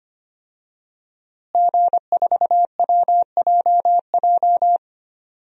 Morse code, heard as a keyed tone, about 25 wpm, 700 Hz. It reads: G4WJJ
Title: DXCC-100 Call Signs - 5 Characters - Repeated 3x - 25wpm